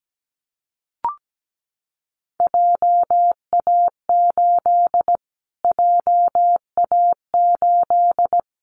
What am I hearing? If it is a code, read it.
Morse code, heard as a keyed tone, about 17 wpm, 700 Hz. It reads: JA8 JA8
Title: Callsign Prefixes for Top 25 DXCC - Repeated 3x - 17wpm